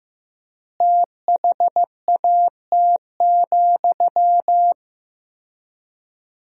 Morse code, heard as a keyed tone, about 16 wpm, 700 Hz. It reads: THAT,
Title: 4x Word-Spacing - Sentences from Top 400 Words - 15wpm